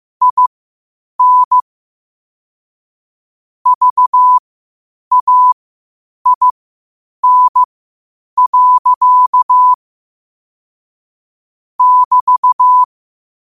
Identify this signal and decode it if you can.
Morse code, heard as a keyed tone, about 15 wpm, spaced as for 8 wpm, 1000 Hz. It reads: IN VAIN. <BT>
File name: qotdfiles / 8wpm